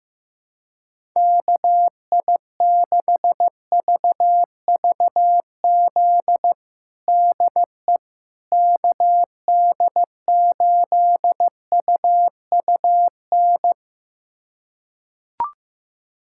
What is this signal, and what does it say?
Morse code, heard as a keyed tone, about 15 wpm, 700 Hz. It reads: KI6VVZ DE KD8UUN
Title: USA Callsign DE Callsign - 6 Characters - 15wpm